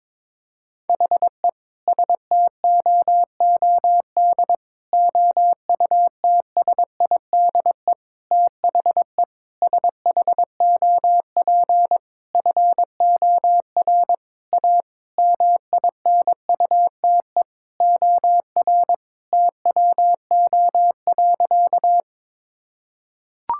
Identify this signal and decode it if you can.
Morse code, heard as a keyed tone, about 22 wpm, 700 Hz. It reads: HE STOOD OUTSIDE THE SHOP FOR A MINUTE OR TWO.
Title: Sentences from Top 1600 Words 22wpm